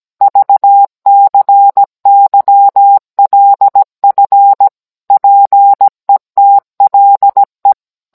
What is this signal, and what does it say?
Morse code, heard as a keyed tone, about 17 wpm, 800 Hz. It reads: VCYLF PETLE